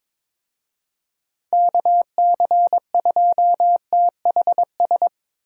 Morse code, heard as a keyed tone, about 25 wpm, 700 Hz. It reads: KC2THS
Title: Speed Racing - USA Call Signs 22wpm